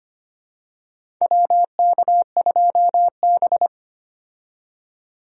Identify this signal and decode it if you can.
Morse code, heard as a keyed tone, about 25 wpm, 700 Hz. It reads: WK2B
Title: USA Call Signs - 4 Characters - 25wpm